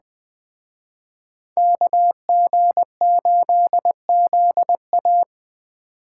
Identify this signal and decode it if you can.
Morse code, heard as a keyed tone, about 20 wpm, 700 Hz. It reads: KG8ZA
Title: Speed Racing - USA 2x2 Call Signs 17wpm